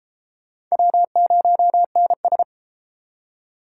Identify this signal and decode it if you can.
Morse code, heard as a keyed tone, about 32 wpm, 700 Hz. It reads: W0NS